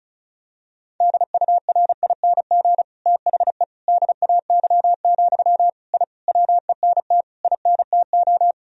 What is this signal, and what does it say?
Morse code, heard as a keyed tone, about 36 wpm, 700 Hz. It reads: DURING THE DAY, I WENT INTO
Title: Sentences from Top 1000 Words - Review Part 2 - 35wpm